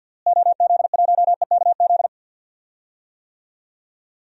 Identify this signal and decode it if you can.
Morse code, heard as a keyed tone, about 52 wpm, 700 Hz. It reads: OZ1EKZ
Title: DXCC-100 Call Signs - 6 Characters - Repeated 3x - 50wpm